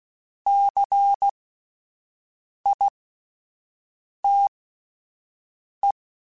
Morse code, heard as a keyed tone, about 16 wpm, spaced as for 5 wpm, 800 Hz. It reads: CITE
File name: Alphabetical / Alphabetical mp3s / Cs @5